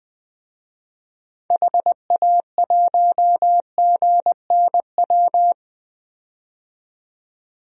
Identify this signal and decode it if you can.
Morse code, heard as a keyed tone, about 20 wpm, 700 Hz. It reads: HA1GNW
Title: DXCC-100 Call Signs - 6 Characters - 20wpm